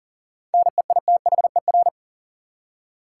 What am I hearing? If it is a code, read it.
Morse code, heard as a keyed tone, about 40 wpm, 700 Hz. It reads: NEITHER
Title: Seven Letter Words - 40wpm